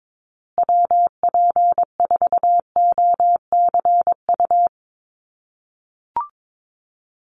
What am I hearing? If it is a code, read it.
Morse code, heard as a keyed tone, about 22 wpm, 700 Hz. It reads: WP4OCU
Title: USA Call Signs Repeated 3x 22wpm